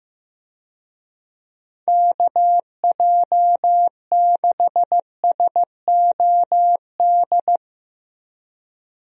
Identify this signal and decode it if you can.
Morse code, heard as a keyed tone, about 15 wpm, 700 Hz. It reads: KJ6SOD
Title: USA Call Signs - 6 Characters - 15wpm